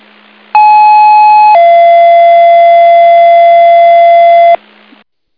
Two-tone page, heard as a one-second tone, then a three-second tone, about 850 then 680 Hz, with noise. It is heard from 0.5 to 4.6 s.